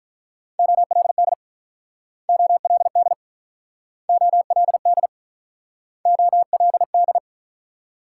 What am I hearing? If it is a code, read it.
Morse code, heard as a keyed tone, about 47 wpm, 700 Hz. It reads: OLD OLD OLD OLD